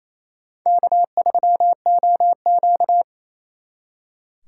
Morse code, heard as a keyed tone, about 35 wpm, 700 Hz. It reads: K3OQ
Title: Speed Racing - USA 1x2 Call Signs 28wpm